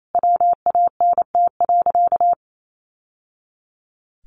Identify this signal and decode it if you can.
Morse code, heard as a keyed tone, about 28 wpm, 700 Hz. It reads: WANT.
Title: Sentences from Top 1100 Words 28wpm